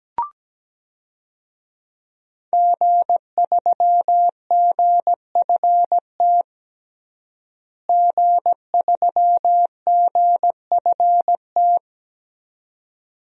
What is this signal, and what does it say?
Morse code, heard as a keyed tone, about 17 wpm, 700 Hz. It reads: G3GFT G3GFT
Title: DXCC-100 Call Signs - 5 Characters - Repeated 2x - 17wpm